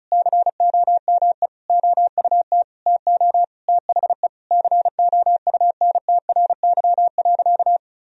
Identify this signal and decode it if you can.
Morse code, heard as a keyed tone, about 36 wpm, 700 Hz. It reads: COME OUT TO THE COUNTRY.
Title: Sentences from Top 1000 Words - Review Part 1 - 35wpm